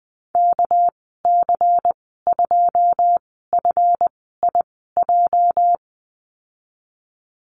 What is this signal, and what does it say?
Morse code, heard as a keyed tone, about 19 wpm, spaced as for 14 wpm, 700 Hz. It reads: KC2FIJ
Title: 219 Lesson 35g - Number 0 - Call Signs 20wpm